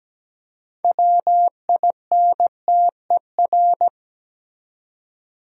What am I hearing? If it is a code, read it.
Morse code, heard as a keyed tone, about 17 wpm, 700 Hz. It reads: WINTER
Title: Six Letter Words - 17wpm